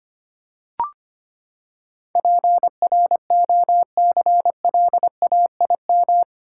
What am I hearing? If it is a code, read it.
Morse code, heard as a keyed tone, about 25 wpm, 700 Hz. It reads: PROCLAIM
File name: Eight Letter Words - 25wpm